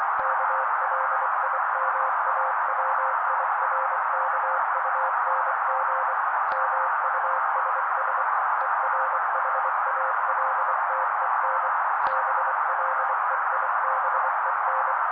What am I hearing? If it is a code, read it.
Morse code, heard as a keyed tone, about 24 wpm, 530 Hz. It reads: KLIMAWIRKUNG MUSS ERHALTEN BLEIBEN
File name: daily / 23wpm noise